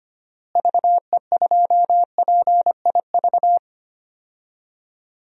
Morse code, heard as a keyed tone, about 25 wpm, 700 Hz. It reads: VE2PIV
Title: DXCC-100 Call Signs - 6 Characters - Repeated 3x - 25wpm